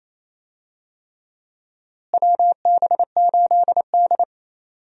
Morse code, heard as a keyed tone, about 28 wpm, 700 Hz. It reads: WB8D
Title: USA Call Signs - 4 Characters - Repeated 3x - 28wpm